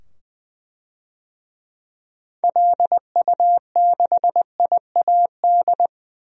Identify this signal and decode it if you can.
Morse code, heard as a keyed tone, about 20 wpm, 700 Hz. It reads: LU6IAD
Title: DXCC-100 Call Signs - 6 Characters - 20wpm